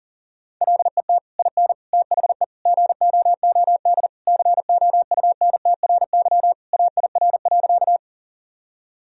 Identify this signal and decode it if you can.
Morse code, heard as a keyed tone, about 40 wpm, 700 Hz. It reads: LET IN THE GOOD COUNTRY AIR.
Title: Sentences from Top 1000 Words - Review Part 1 - 40wpm